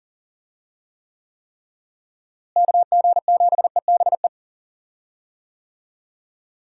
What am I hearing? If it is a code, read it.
Morse code, heard as a keyed tone, about 39 wpm, 700 Hz. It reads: KG7EBE
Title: USA Call Signs - 6 Characters - Repeated 3x - 40wpm